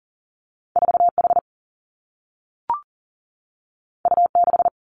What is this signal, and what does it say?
Morse code, heard as a keyed tone, about 40 wpm, 700 Hz. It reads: 4 H U 6
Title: Two Challenging Letter-Number Combinations 40wpm